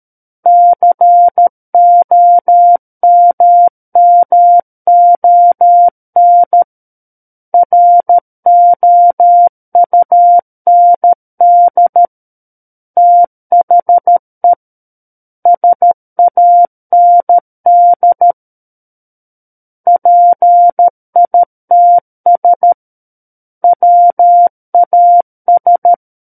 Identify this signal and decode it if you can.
Morse code, heard as a keyed tone, about 13 wpm, 700 Hz. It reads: COMMON ROUND THE SAND PITS WAS
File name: Book One / 04-Chapter 4 - 13 WPM